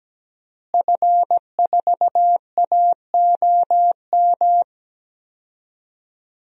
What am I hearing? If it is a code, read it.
Morse code, heard as a keyed tone, about 17 wpm, 700 Hz. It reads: F4AOM
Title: DXCC-100 Call Signs - 5 Characters - Repeated 2x - 17wpm